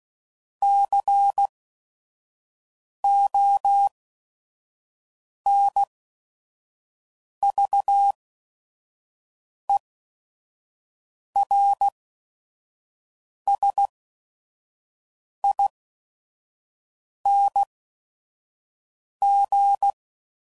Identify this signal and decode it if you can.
Morse code, heard as a keyed tone, about 16 wpm, spaced as for 5 wpm, 800 Hz. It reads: CONVERSING
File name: Suffixes / Special ING's / VerySlowToFaster ING's